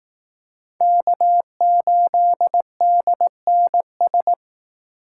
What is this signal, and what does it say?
Morse code, heard as a keyed tone, about 18 wpm, 700 Hz. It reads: K8DNS
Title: Speed Racing - USA 1x3 Call Signs 15wpm